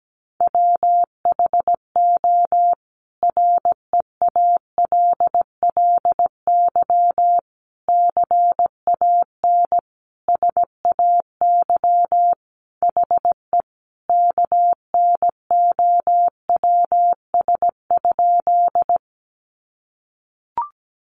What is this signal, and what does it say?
Morse code, heard as a keyed tone, about 17 wpm, 700 Hz. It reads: WHO REALLY CAN SAY HE KNOWS?
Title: A Sound Of Thunder 17wpm